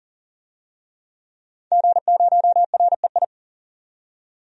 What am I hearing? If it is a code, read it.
Morse code, heard as a keyed tone, about 40 wpm, 700 Hz. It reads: G0REI
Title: DXCC-100 Call Signs - 5 Characters - 40wpm